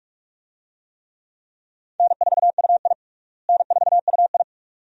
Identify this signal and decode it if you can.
Morse code, heard as a keyed tone, about 46 wpm, 700 Hz. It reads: N4UI N4UI
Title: USA Call Signs - 4 Characters - Repeated 2x - 45wpm